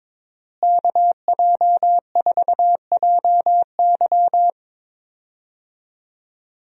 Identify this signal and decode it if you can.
Morse code, heard as a keyed tone, about 21 wpm, 700 Hz. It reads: KJ4JY
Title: USA Call Signs - 5 Characters - Repeated 2x - 22wpm